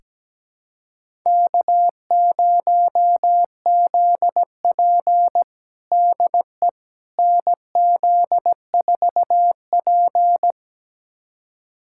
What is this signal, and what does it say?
Morse code, heard as a keyed tone, about 17 wpm, 700 Hz. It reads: K0ZP DE NZ4P
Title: USA Callsign DE Callsign - 4 Characters - 17wpm